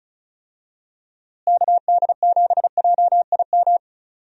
Morse code, heard as a keyed tone, about 36 wpm, 700 Hz. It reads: KD7JIM
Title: USA Call Signs - 6 Characters - Repeated 3x - 35wpm